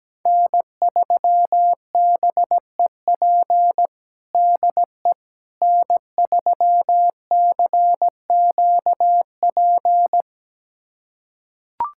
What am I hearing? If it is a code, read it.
Morse code, heard as a keyed tone, about 17 wpm, 700 Hz. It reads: N3BEP DE N3CQP
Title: USA Callsign DE Callsign - 5 Characters - 17wpm